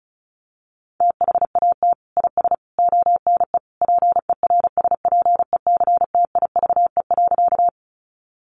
Morse code, heard as a keyed tone, about 34 wpm, 700 Hz. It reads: THAT IS ONE PERSPECTIVE.
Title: Sentences from Top 1200 Words 35wpm